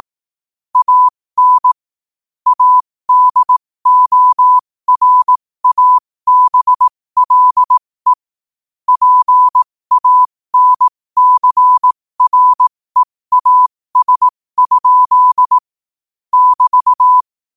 Morse code, heard as a keyed tone, about 18 wpm, spaced as for 15 wpm, 1000 Hz. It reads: AN ADORABLE PANCREAS? <BT>
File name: qotdfiles / 15wpm